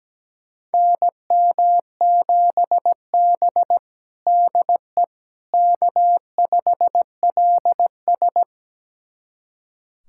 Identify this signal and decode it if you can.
Morse code, heard as a keyed tone, about 17 wpm, 700 Hz. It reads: NM7B DE K5LS
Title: USA Callsign DE Callsign - 4 Characters - 17wpm